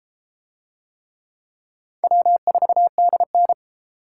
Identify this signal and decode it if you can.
Morse code, heard as a keyed tone, about 30 wpm, 700 Hz. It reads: W4DN